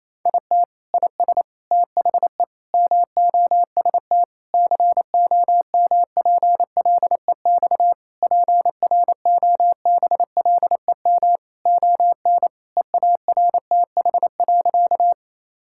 Morse code, heard as a keyed tone, about 28 wpm, 700 Hz. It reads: IT IS THE MOST COMPLEX PROBLEM ON EARTH.
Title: Sentences from Top 1000 Words - Review Part 6 - 28wpm